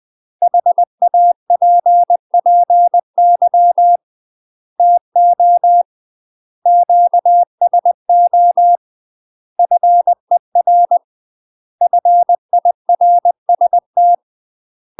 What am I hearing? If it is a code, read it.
Morse code, heard as a keyed tone, about 20 wpm, 700 Hz. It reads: HAPPY TO QSO FER FIRST
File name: UK MASTER QSOs TRAINING / 20WPM / UK MASTER QSOs 0036